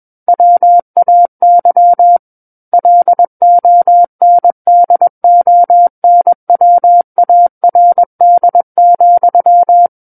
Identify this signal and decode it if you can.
Morse code, heard as a keyed tone, about 21 wpm, 700 Hz. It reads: WAY LONDONWARD,
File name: Book One / 12-Chapter 12 - 21 WPM